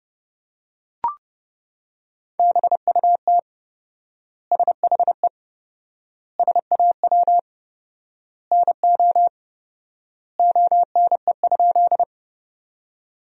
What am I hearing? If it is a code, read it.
Morse code, heard as a keyed tone, about 30 wpm, 700 Hz. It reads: BUT SHE SAW NO ONE?
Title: 4x Word-Spacing - Sentences from Top 300 Words - 30wpm